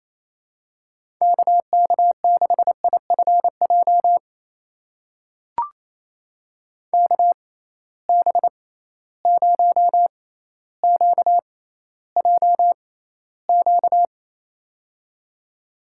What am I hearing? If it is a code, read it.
Morse code, heard as a keyed tone, about 28 wpm, spaced as for 14 wpm, 700 Hz. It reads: KK6IFJ KB0QJQ
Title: USA Call Signs - 6x Character Spacing - 28wpm